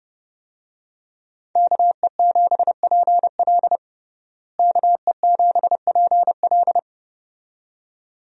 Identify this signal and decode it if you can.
Morse code, heard as a keyed tone, about 30 wpm, 700 Hz. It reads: KE7PL KE7PL
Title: USA Call Signs - 5 Characters - Repeated 2x - 30wpm